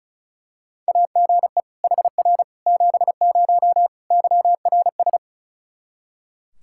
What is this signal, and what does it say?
Morse code, heard as a keyed tone, about 34 wpm, 700 Hz. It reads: AGE HR 70 YRS
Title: Practice QSO - Exchange 3 - More Info - 35wpm